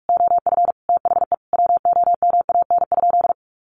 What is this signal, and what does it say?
Morse code, heard as a keyed tone, about 44 wpm, 700 Hz. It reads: OF THE WOMAN?
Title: Sentences from Top 1400 Words 45wpm